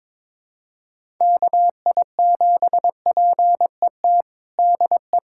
Morse code, heard as a keyed tone, about 22 wpm, 700 Hz. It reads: KI7PET DE
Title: USA Callsign DE Callsign - 6 Characters - 22wpm